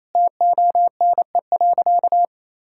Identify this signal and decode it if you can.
Morse code, heard as a keyed tone, about 28 wpm, 700 Hz. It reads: TONE.
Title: Sentences from Top 1600 Words 28wpm